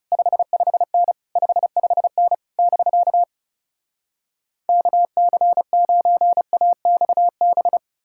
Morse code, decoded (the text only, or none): 55N 55N <BK> KC9AXB